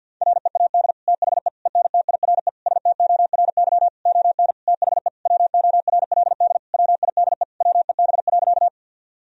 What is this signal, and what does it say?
Morse code, heard as a keyed tone, about 50 wpm, 700 Hz. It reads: READ THE ENTIRE STORY ON THE WORLD WIDE WEB.
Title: Sentences from Top 1000 Words - Review Part 4 - 50wpm